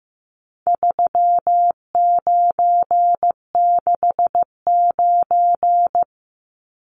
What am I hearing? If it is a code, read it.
Morse code, heard as a keyed tone, about 15 wpm, 700 Hz. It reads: 3969
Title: Numbers - 4-Digit 15wpm